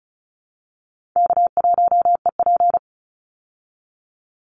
Morse code, heard as a keyed tone, about 34 wpm, 700 Hz. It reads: K1EP
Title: Easy Call Signs Repeated 3x - 35wpm